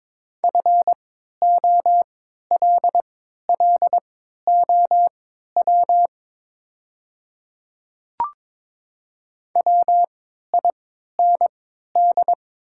Morse code, heard as a keyed tone, about 22 wpm, spaced as for 12 wpm, 700 Hz. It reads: FOLLOW WIND
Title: 3x Character Spacing - Top 500 Words - 22wpm